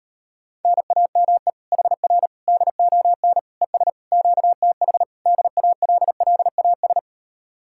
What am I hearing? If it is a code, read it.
Morse code, heard as a keyed tone, about 39 wpm, 700 Hz. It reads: NAME HR DON ES QTH DALLAS